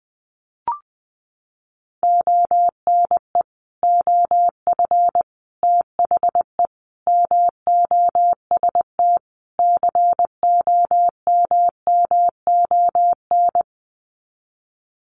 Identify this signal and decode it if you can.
Morse code, heard as a keyed tone, about 20 wpm, 700 Hz. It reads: ONE OF THE MOST COMMON
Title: Sets of 5 Words Encore2 20wpm